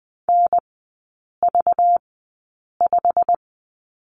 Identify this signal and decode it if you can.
Morse code, heard as a keyed tone, about 20 wpm, spaced as for 8 wpm, 700 Hz. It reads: NV5
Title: 191 Lesson 31e - Letter v - Three Characters 20wpm